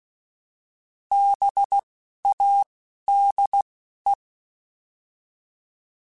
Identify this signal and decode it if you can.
Morse code, heard as a keyed tone, about 16 wpm, 800 Hz. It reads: B A D E